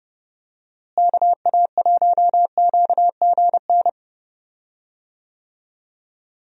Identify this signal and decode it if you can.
Morse code, heard as a keyed tone, about 30 wpm, 700 Hz. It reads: KA1QGN
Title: USA Call Signs - 6 Characters - Repeated 3x - 30wpm